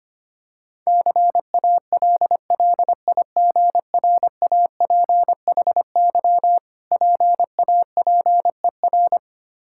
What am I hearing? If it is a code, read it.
Morse code, heard as a keyed tone, about 25 wpm, 690 Hz. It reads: CALLIGRAPHY PAPER
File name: practice / 25wpm / Difficult Words with Hints - 25wpm